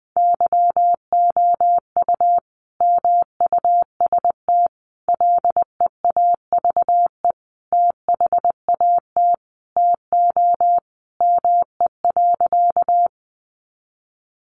Morse code, heard as a keyed tone, about 20 wpm, 700 Hz. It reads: YOU MUST LEAVE THAT TO ME.
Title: Sentences from Top 400 Words 20wpm